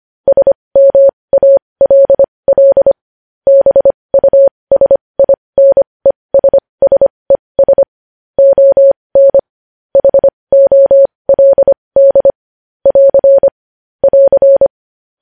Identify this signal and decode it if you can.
Morse code, heard as a keyed tone, about 25 wpm, spaced as for 20 wpm, 550 Hz. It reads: SMALL BUSINESSES ON HOLD <AR> <AR>
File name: podcast / News headlines-1x-25x20